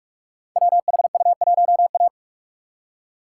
Morse code, heard as a keyed tone, about 44 wpm, 690 Hz. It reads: WHU1A
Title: Code Groups - 5 Letter:Numbers - 45wpm